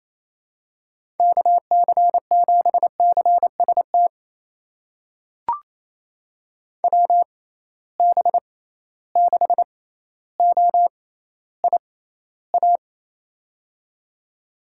Morse code, heard as a keyed tone, about 28 wpm, spaced as for 14 wpm, 700 Hz. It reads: KC7CST WB6OIA
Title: USA Call Signs - 6 Characters - 6x Character Spacing - 28wpm